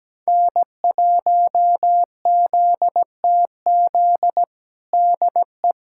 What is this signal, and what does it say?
Morse code, heard as a keyed tone, about 17 wpm, 700 Hz. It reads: N1ZTZ DE
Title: USA Callsign DE Callsign 17wpm